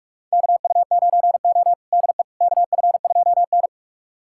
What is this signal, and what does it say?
Morse code, heard as a keyed tone, about 46 wpm, 700 Hz. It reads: KU9O DE KF2N
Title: USA Callsign DE Callsign - 4 Characters - 45wpm